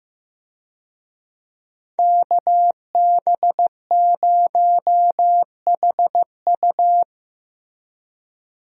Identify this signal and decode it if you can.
Morse code, heard as a keyed tone, about 15 wpm, 700 Hz. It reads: KB0HU
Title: USA Call Signs - 5 Characters - Repeated 2x - 15wpm